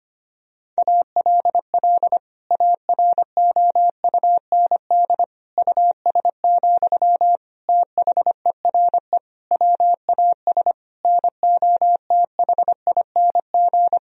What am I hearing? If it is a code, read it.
Morse code, heard as a keyed tone, about 25 wpm, 700 Hz. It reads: ALL AROUND US, THERE WAS NOTHING
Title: Sentences from Top 1000 Words - Review Part 3 - 25wpm